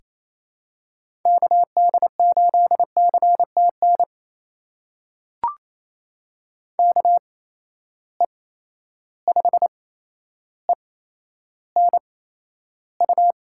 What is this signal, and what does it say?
Morse code, heard as a keyed tone, about 28 wpm, spaced as for 12 wpm, 700 Hz. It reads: KD8CTN KE5ENU
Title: USA Call Signs - 6 Characters - 8x Character Spacing - 28wpm